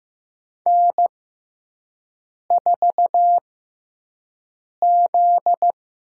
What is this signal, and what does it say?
Morse code, heard as a keyed tone, about 15 wpm, spaced as for 5 wpm, 700 Hz. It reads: N4Z
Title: USA Call Signs - 5 Characters - 6x Character Spacing - 15wpm